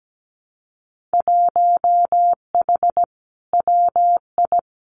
Morse code, heard as a keyed tone, about 17 wpm, 700 Hz. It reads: 1H WI
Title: Winter Field Day Exchanges 17wpm